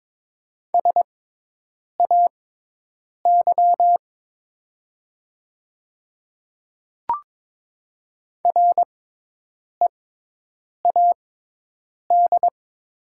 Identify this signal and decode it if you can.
Morse code, heard as a keyed tone, about 22 wpm, spaced as for 8 wpm, 700 Hz. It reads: SAY READ
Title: 6x Character Spacing - Top 200 Words - 22wpm